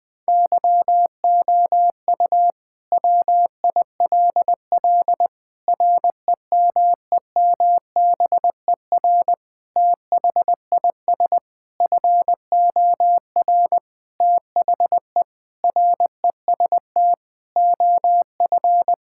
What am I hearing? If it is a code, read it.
Morse code, heard as a keyed tone, about 19 wpm, 700 Hz. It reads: YOU WILL REMEMBER THIS FOR THE REST OF
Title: Sentences with 5-Word Phrases - Spoken Beginning - 20wpm